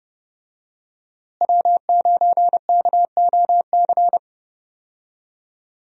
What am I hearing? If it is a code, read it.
Morse code, heard as a keyed tone, about 30 wpm, 700 Hz. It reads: W9KOC
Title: USA Call Signs - 5 Characters - Repeated 3x - 30wpm